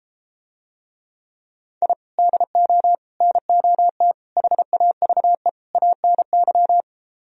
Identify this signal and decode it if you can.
Morse code, heard as a keyed tone, about 33 wpm, 700 Hz. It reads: I DO NOT HAVE ANY